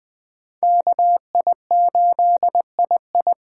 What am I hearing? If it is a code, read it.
Morse code, heard as a keyed tone, about 20 wpm, 700 Hz. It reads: KI8II
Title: USA Call Signs - 5 Characters - 4x Character Spacing - 20wpm